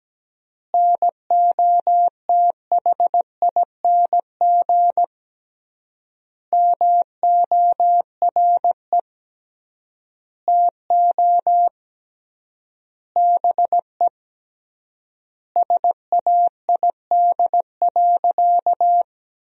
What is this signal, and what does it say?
Morse code, heard as a keyed tone, about 17 wpm, 700 Hz. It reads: NOTHING MORE TO BE SAID.
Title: 3x Word-Spacing - Sentences from Top 500 Words - 17wpm